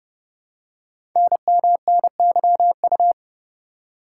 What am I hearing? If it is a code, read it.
Morse code, heard as a keyed tone, about 30 wpm, 700 Hz. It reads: NMNYU
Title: Code Groups - 5 Letters - 30wpm